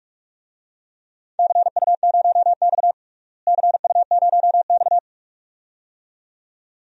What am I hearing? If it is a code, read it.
Morse code, heard as a keyed tone, about 47 wpm, 700 Hz. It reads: CU0X CU0X
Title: DXCC-100 Call Signs - 4 Characters - Repeated 2x - 45wpm